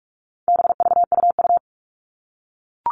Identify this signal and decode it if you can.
Morse code, heard as a keyed tone, about 44 wpm, 700 Hz. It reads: BVUU
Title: All Cut Numbers - 4-Digit 45wpm